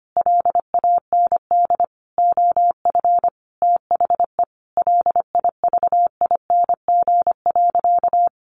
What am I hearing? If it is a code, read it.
Morse code, heard as a keyed tone, about 25 wpm, 700 Hz. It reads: LAND OF THE LIVING.
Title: Sentences from Top 1100 Words 25wpm